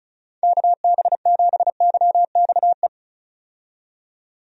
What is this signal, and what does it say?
Morse code, heard as a keyed tone, about 34 wpm, 700 Hz. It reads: KB7YXE
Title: USA Call Signs - 6 Characters - 35wpm